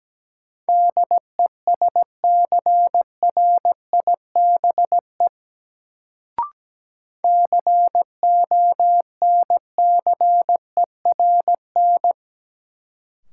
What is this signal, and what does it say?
Morse code, heard as a keyed tone, about 17 wpm, 700 Hz. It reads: DESCRIBE CONCERN
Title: Sentences from Top 1000 Words - New Word Drill - 17wpm